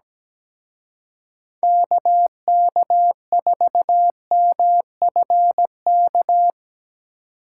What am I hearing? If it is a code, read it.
Morse code, heard as a keyed tone, about 18 wpm, 700 Hz. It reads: KK4MFK
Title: Speed Racing - USA 2x3 Call Signs 17wpm